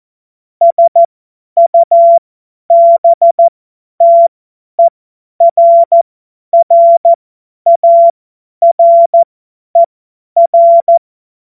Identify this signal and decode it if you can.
Morse code, heard as a keyed tone, about 14 wpm, spaced as for 10 wpm, 680 Hz. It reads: SUBTERRARER
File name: Morsekurs / Witze / Witz 04. Mai 2021, 50 F70